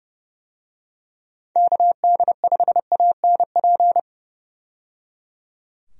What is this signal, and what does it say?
Morse code, heard as a keyed tone, about 30 wpm, 700 Hz. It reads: KD5ANP
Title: USA Call Signs - 6 Characters - 30wpm